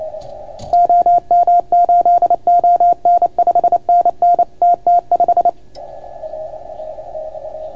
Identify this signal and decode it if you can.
Morse code, heard as a keyed tone, about 28 wpm, 690 Hz. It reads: OM8ON 5NN TT5